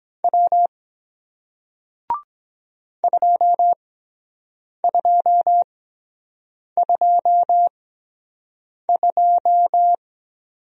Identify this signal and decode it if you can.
Morse code, heard as a keyed tone, about 23 wpm, spaced as for 7 wpm, 700 Hz. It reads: W 2222